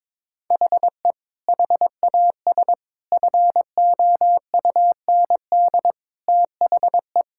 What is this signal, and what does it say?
Morse code, heard as a keyed tone, about 22 wpm, 700 Hz. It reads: HE HAS FOUND THE
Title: Sentences from Top 1000 Words - Review Part 5 - 22wpm